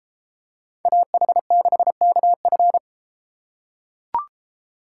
Morse code, heard as a keyed tone, about 31 wpm, 700 Hz. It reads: AH6KF